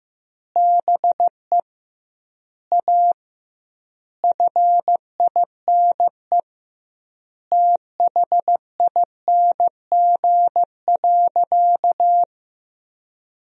Morse code, heard as a keyed tone, about 15 wpm, 700 Hz. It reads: BE A FINE THING.
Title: 2x Word-Spacing - Sentences from Top 500 Words - 15wpm